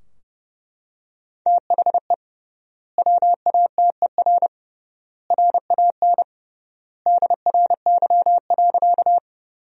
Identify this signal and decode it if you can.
Morse code, heard as a keyed tone, about 30 wpm, 700 Hz. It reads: THE WATER RAN DRY.
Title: 3x Word-Spacing - Sentences from Top 500 Words - 30wpm